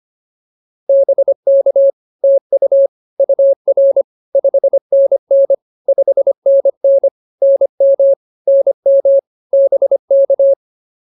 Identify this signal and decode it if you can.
Morse code, heard as a keyed tone, about 25 wpm, 560 Hz. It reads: BK TU UR 5NN 5NN NM NM BK